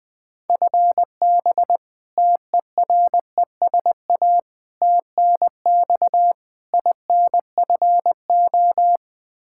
Morse code, heard as a keyed tone, about 20 wpm, 700 Hz. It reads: FB TERESA TNX INFO
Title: Practice QSO - Exchange 3 - More Info - 20wpm